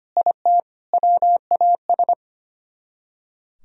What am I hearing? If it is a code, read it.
Morse code, heard as a keyed tone, about 33 wpm, 700 Hz. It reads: IT WAS